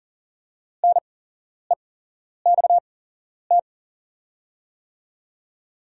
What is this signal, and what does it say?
Morse code, heard as a keyed tone, about 40 wpm, spaced as for 11 wpm, 690 Hz. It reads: NEXT
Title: 8x Character Spacing - Top 500 Words - 40wpm